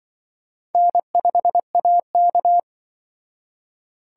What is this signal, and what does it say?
Morse code, heard as a keyed tone, about 25 wpm, 700 Hz. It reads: N5AK
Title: Speed Racing - USA 1x2 Call Signs 20wpm